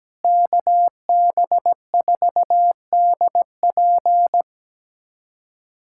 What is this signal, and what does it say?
Morse code, heard as a keyed tone, about 17 wpm, 700 Hz. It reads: KB4DP
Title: USA Callsign DE Callsign - 5 Characters - 17wpm